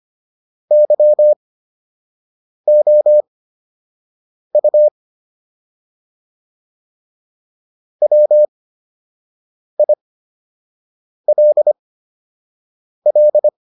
Morse code, heard as a keyed tone, about 25 wpm, spaced as for 6 wpm, 600 Hz. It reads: YOU WILL